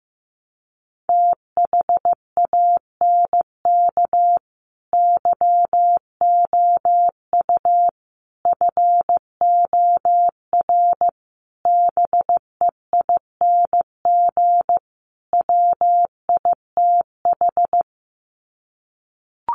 Morse code, heard as a keyed tone, about 15 wpm, 700 Hz. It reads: THANK YOU FOR BEING WITH
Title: Sets of 5 Words 15wpm